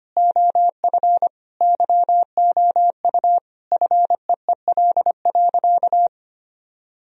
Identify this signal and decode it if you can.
Morse code, heard as a keyed tone, about 25 wpm, 700 Hz. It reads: OF YOU FEEL.
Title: Sentences from Top 1000 Words - Review Part 1 - 25wpm